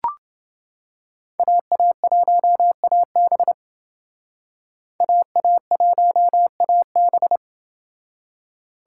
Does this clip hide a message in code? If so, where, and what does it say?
Yes: Morse code from 1.4 to 7.4 s, heard as a keyed tone, about 29 wpm, 700 Hz. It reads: AA1AB AA1AB